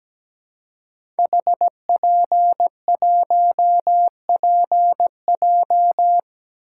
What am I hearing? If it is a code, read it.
Morse code, heard as a keyed tone, about 17 wpm, 700 Hz. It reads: HP1PJ
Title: Code Groups - 5 Letter:Numbers - 17wpm